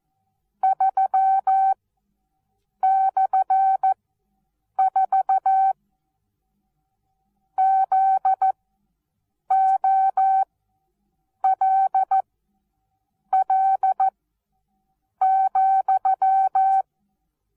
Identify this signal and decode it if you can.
Morse code, heard as a keyed tone, about 14 wpm, spaced as for 7 wpm, 770 Hz. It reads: 3/4 ZOLL,
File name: Onlinelehrgang / morsen / DJ4UF Morsen19